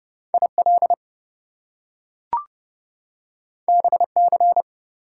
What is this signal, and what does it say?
Morse code, heard as a keyed tone, about 31 wpm, 700 Hz. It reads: IL BC
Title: US States and Canadian Provinces 30wpm